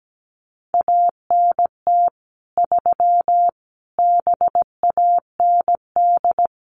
Morse code, heard as a keyed tone, about 17 wpm, 700 Hz. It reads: ANT 3 BAND
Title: Common QSO Phrases 17wpm